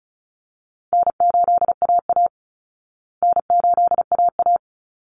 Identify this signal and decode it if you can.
Morse code, heard as a keyed tone, about 34 wpm, 700 Hz. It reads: N8AA N8AA
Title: Call Signs Repeated 3x - 35wpm